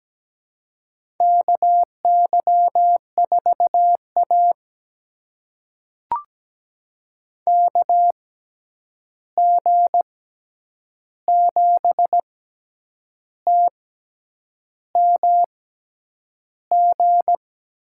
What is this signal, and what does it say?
Morse code, heard as a keyed tone, about 17 wpm, spaced as for 8 wpm, 700 Hz. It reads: KY4A KG7TMG
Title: USA Call Signs - 6x Character Spacing - 17wpm